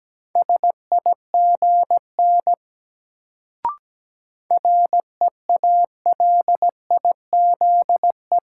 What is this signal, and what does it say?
Morse code, heard as a keyed tone, about 17 wpm, 700 Hz. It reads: SIGN REALIZE
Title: Sentences from Top 1000 Words - New Word Drill - 17wpm